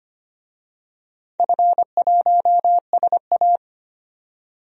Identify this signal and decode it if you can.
Morse code, heard as a keyed tone, about 25 wpm, 700 Hz. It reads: F1SA
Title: DXCC-100 Call Signs - 4 Characters - 25wpm